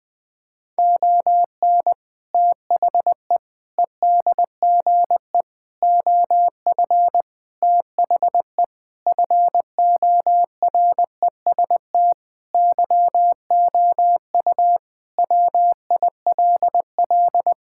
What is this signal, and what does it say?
Morse code, heard as a keyed tone, about 20 wpm, 700 Hz. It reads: ON THE EDGE OF THE FOREST YOU WILL
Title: Sentences with 4-Word Phrases - 20wpm